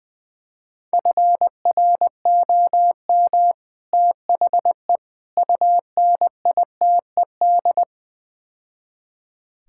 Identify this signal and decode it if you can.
Morse code, heard as a keyed tone, about 21 wpm, 700 Hz. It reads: FROM THE UNITED
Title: Build A Phrase - 4-Word Phrases - 20wpm